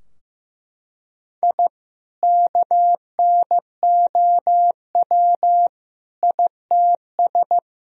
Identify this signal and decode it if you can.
Morse code, heard as a keyed tone, about 15 wpm, 700 Hz. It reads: I KNOW ITS
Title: Sentences from Top 1000 Words - Review Part 2 - 15wpm